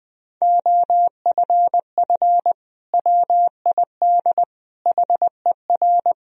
Morse code, heard as a keyed tone, about 20 wpm, 700 Hz. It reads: OFF WID HER
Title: Sentences from Top 300 Words and Abbreviations 20wpm